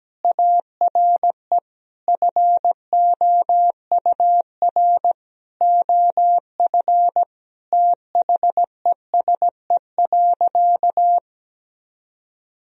Morse code, heard as a keyed tone, about 17 wpm, 700 Hz. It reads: ARE FOUR OF THESE.
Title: Sentences from Top 1000 Words - Review Part 1 - 17wpm